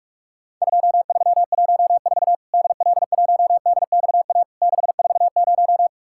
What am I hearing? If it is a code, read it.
Morse code, heard as a keyed tone, about 44 wpm, 700 Hz. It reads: 2314 DL1DXA 640
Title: Worked All Europe DX Contest - QTC Exchange - 45wpm